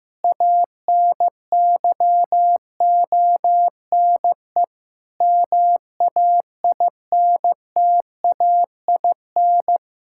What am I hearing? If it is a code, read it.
Morse code, heard as a keyed tone, about 15 wpm, 700 Hz. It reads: ANYONE MAINTAIN
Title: Sentences from Top 1000 Words - Review Part 5 - 15wpm